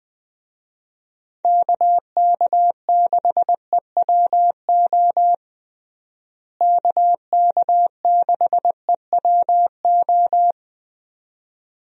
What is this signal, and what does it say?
Morse code, heard as a keyed tone, about 20 wpm, 700 Hz. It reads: KK6EWO KK6EWO
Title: USA Call Signs - 6 Characters - Repeated 2x - 20wpm